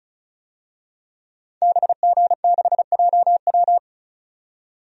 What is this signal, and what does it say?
Morse code, heard as a keyed tone, about 34 wpm, 700 Hz. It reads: BG6JW
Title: DXCC-100 Call Signs - 5 Characters - 35wpm